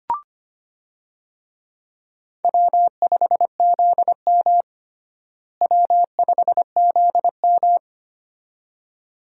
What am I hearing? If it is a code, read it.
Morse code, heard as a keyed tone, about 25 wpm, 700 Hz. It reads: W5ZM W5ZM
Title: USA Call Signs - 4 Characters - Repeated 2x - 25wpm